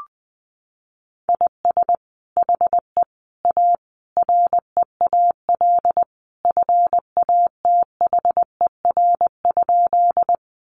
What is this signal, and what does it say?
Morse code, heard as a keyed tone, about 20 wpm, 700 Hz. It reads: IS HE A REAL FATHER?
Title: Sentences from Top 300 Words 20wpm